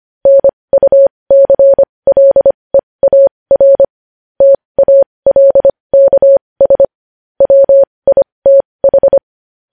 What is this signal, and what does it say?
Morse code, heard as a keyed tone, about 25 wpm, spaced as for 20 wpm, 550 Hz. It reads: NUCLEAR TALKS WITH
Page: news headlines in morse code at 20 wpm